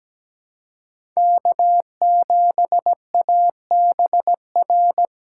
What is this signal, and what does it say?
Morse code, heard as a keyed tone, about 17 wpm, 700 Hz. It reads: K7ABR
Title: USA Call Signs - 5 Characters - 4x Character Spacing - 17wpm